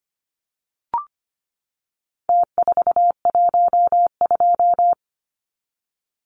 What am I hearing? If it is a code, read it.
Morse code, heard as a keyed tone, about 25 wpm, 700 Hz. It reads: T412
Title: Numbers - Contest Style - 4-Digit 25wpm